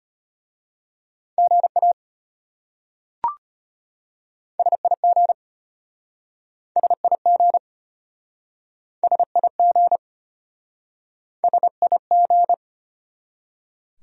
Morse code, heard as a keyed tone, about 33 wpm, 700 Hz. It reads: GA SIG SIG SIG SIG